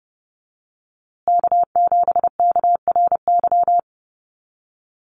Morse code, heard as a keyed tone, about 30 wpm, 700 Hz. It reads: K7KRY
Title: USA Call Signs Repeated 2x 30wpm